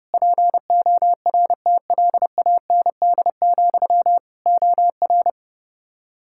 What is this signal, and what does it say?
Morse code, heard as a keyed tone, about 30 wpm, 700 Hz. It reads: PORTLAND, OR
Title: Practice QSO - Exchange 1 - Beginning - 30wpm